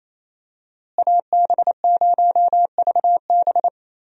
Morse code, heard as a keyed tone, about 28 wpm, 700 Hz. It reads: AB0VB
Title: USA Call Signs - 5 Characters - Repeated 2x - 28wpm